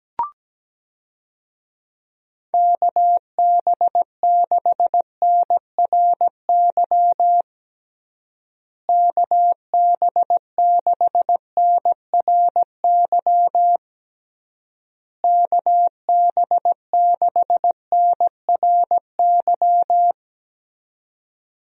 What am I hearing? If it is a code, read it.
Morse code, heard as a keyed tone, about 17 wpm, 700 Hz. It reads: KB6NRY KB6NRY KB6NRY
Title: USA Call Signs - 6 Characters - Repeated 3x - 17wpm